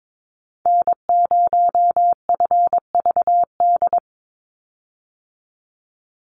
Morse code, heard as a keyed tone, about 22 wpm, 700 Hz. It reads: N0FVD
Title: USA Call Signs 22wpm